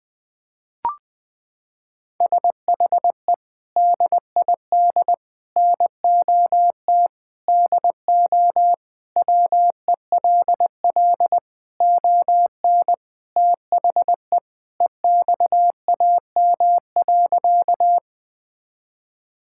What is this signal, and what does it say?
Morse code, heard as a keyed tone, about 21 wpm, 700 Hz. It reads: SHE DID NOT DO WELL ON THE EXAM.
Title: Sentences from Top 1000 Words - Review Part 5 - 20wpm